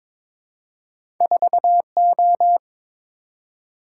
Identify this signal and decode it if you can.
Morse code, heard as a keyed tone, about 23 wpm, 700 Hz. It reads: 4O
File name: practice / 22wpm / Two Letter-Number Combinations - 22wpm